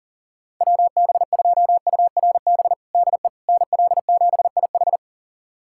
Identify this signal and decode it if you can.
Morse code, heard as a keyed tone, about 40 wpm, 700 Hz. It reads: WB2URB DE NL7IH
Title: USA Callsign DE Callsign 40wpm